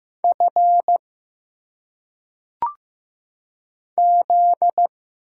Morse code, heard as a keyed tone, about 15 wpm, 700 Hz. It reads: F Z
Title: Single Letters - Phonetic Pronunciation 15wpm